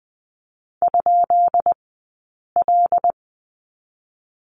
Morse code, heard as a keyed tone, about 20 wpm, spaced as for 8 wpm, 700 Hz. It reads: ?L
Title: 144 Lesson 24d - Letter y - Two Characters 20wpm